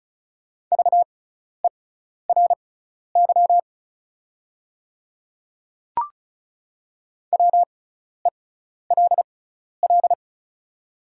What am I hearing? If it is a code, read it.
Morse code, heard as a keyed tone, about 36 wpm, spaced as for 12 wpm, 700 Hz. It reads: VERY WELL
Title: 6x Character Spacing - Top 200 Words - 35wpm